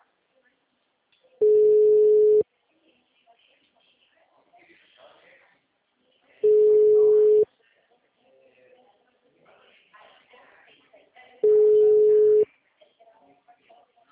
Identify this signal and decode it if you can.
Ringback tone (Europe, 425 Hz), with speech.
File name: out- -2005- - - .33964